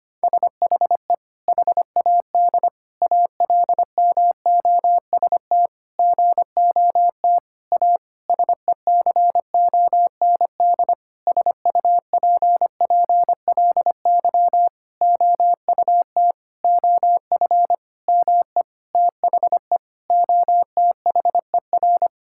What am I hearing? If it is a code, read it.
Morse code, heard as a keyed tone, about 25 wpm, 700 Hz. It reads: SHE HAD ALMOST GOT A SECOND SUPPLY OUT OF ME THE OTHER
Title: Sentences from Top 1000 Words - Review Part 5 - 25wpm